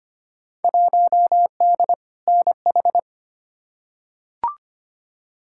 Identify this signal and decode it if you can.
Morse code, heard as a keyed tone, about 25 wpm, 700 Hz. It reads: 1D NH
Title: Field Day Exchanges - 25wpm